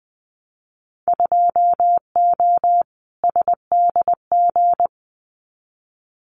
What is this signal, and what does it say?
Morse code, heard as a keyed tone, about 20 wpm, 700 Hz. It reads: 2O SDG
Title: Winter Field Day Exchanges 20wpm